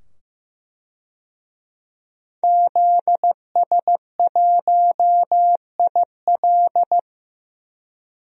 Morse code, heard as a keyed tone, about 15 wpm, 700 Hz. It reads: ZS1IL
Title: DXCC-100 Call Signs - 5 Characters - 15wpm